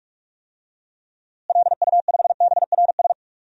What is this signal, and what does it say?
Morse code, heard as a keyed tone, about 41 wpm, 700 Hz. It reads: LU5BRS
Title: DXCC-100 Call Signs - 6 Characters - Repeated 3x - 45wpm